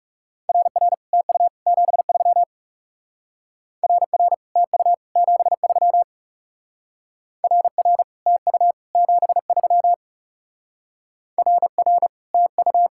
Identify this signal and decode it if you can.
Morse code, heard as a keyed tone, about 37 wpm, 700 Hz. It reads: RR TU 73 RR TU 73 RR TU 73 RR TU
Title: Speed-Racing Common QSO Phrases - 30wpm